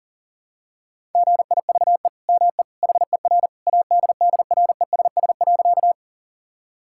Morse code, heard as a keyed tone, about 40 wpm, 700 Hz. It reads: GIVE ME HER ADDRESS.
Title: Sentences from Top 1000 Words - Review Part 6 - 40wpm